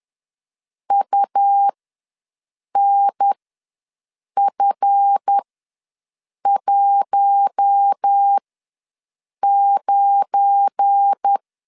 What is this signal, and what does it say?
Morse code, heard as a keyed tone, about 10 wpm, spaced as for 6 wpm, 800 Hz. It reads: UNF19
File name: LEZIONI / 7 GF9Z / 7 GF9 ws 10wpm 4034